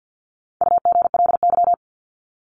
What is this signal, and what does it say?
Morse code, heard as a keyed tone, about 50 wpm, 700 Hz. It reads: UGLY
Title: Top 1751-2000 Words 50wpm